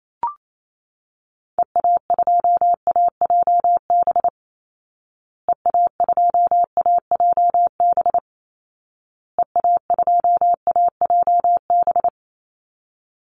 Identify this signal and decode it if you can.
Morse code, heard as a keyed tone, about 28 wpm, 700 Hz. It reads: EA2AJB EA2AJB EA2AJB
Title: Call Signs Repeated 2x - 28wpm